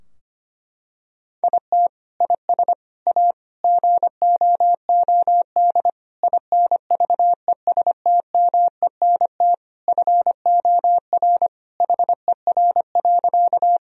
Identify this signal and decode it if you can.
Morse code, heard as a keyed tone, about 25 wpm, 700 Hz. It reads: IT IS A GOOD INVESTMENT FOR HER.
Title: Sentences from Top 1000 Words - Review Part 4 - 25wpm